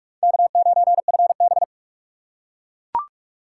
Morse code, heard as a keyed tone, about 38 wpm, 700 Hz. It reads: K9FB